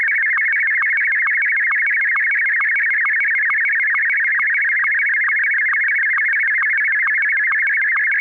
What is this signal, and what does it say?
SSTV (slow-scan television), Martin 1.